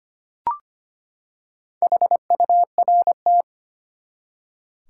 Morse code, heard as a keyed tone, about 25 wpm, 700 Hz. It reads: HURT
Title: Four Letter Words - 25wpm